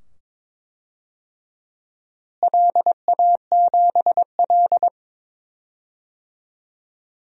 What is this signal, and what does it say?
Morse code, heard as a keyed tone, about 21 wpm, 700 Hz. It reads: LA7L
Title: DXCC-100 Call Signs - 4 Characters - Repeated 2x - 22wpm